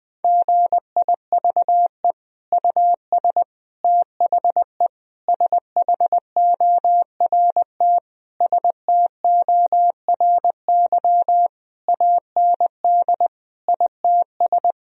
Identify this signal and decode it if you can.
Morse code, heard as a keyed tone, about 20 wpm, 700 Hz. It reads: GIVE US THE SHORT STORY AND ITS
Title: Sentences from Top 1000 Words - Review Part 4 - 20wpm